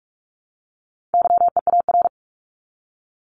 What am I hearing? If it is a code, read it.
Morse code, heard as a keyed tone, about 44 wpm, 700 Hz. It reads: YEAR
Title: ES:S Suffix - Top 100 Words with Base Words 45wpm